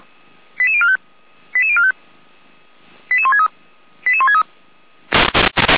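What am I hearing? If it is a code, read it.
Five-tone selective call (ZVEI), heard as quick runs of five tones, tones of about 70 ms. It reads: 88045 88045 88163 88163
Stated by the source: ZVEI five-tone call received on SDR.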